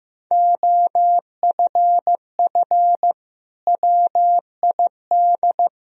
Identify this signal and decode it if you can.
Morse code, heard as a keyed tone, about 16 wpm, 700 Hz. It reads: OFF WID
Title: Sentences from Top 300 Words and Abbreviations 15wpm